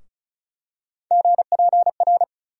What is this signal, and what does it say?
Morse code, heard as a keyed tone, about 36 wpm, 700 Hz. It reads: GPR
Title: USA Call Sign Postfixes - 3 Letters 35wpm